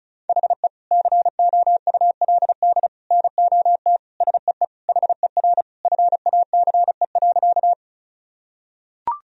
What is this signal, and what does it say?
Morse code, heard as a keyed tone, about 36 wpm, 700 Hz. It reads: HE COULD NOT SEE HER FACE.
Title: Sentences from Top 1000 Words - Review Part 2 - 35wpm